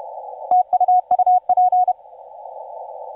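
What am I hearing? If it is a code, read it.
Morse code, heard as a keyed tone, about 31 wpm, 730 Hz, with radio noise. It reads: TU UP
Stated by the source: TU7C on 20 metre CW, DX station recording by IZ8GNR.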